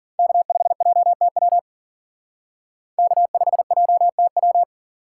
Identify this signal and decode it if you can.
Morse code, heard as a keyed tone, about 51 wpm, 700 Hz. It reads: K5JTW K5JTW